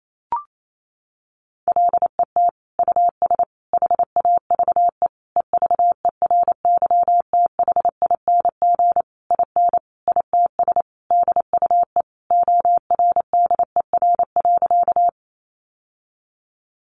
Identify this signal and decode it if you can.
Morse code, heard as a keyed tone, about 28 wpm, 700 Hz. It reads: LET US HAVE EVERYTHING IN ITS DUE ORDER.
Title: Sentences from Top 900 Words 28wpm